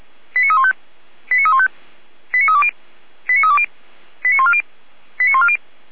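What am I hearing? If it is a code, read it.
Five-tone selective call (ZVEI), heard as quick runs of five tones, tones of about 70 ms. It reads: 89315 89315 89320 89320 89140 89140
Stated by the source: ZVEI five-tone selective call received with an SDR, on the audio server of Feuerwehr Orlamünde (fire brigade).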